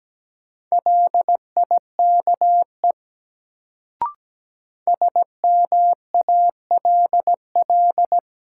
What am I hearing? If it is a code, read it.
Morse code, heard as a keyed tone, about 17 wpm, 700 Hz. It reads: LIKE SMALL
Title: Sets of 2 Words - New Word Drill - 17wpm